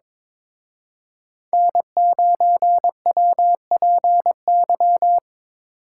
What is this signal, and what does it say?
Morse code, heard as a keyed tone, about 26 wpm, 700 Hz. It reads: N9WPY
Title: Speed Racing - USA Call Signs 22wpm